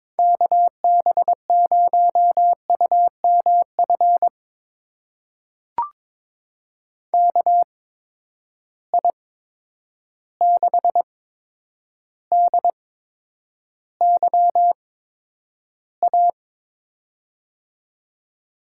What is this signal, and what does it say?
Morse code, heard as a keyed tone, about 22 wpm, spaced as for 9 wpm, 700 Hz. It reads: KB0UMF KI6DYA
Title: USA Call Signs - 6 Characters - 8x Character Spacing - 22wpm